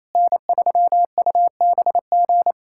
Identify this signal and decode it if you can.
Morse code, heard as a keyed tone, about 28 wpm, 700 Hz. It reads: N3UBG
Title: USA Callsign DE Callsign - 5 Characters - 28wpm